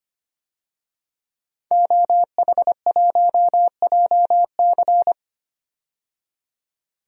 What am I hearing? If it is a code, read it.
Morse code, heard as a keyed tone, about 25 wpm, 690 Hz. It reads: OH1JC
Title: DXCC-100 Call Signs - 5 Characters - 25wpm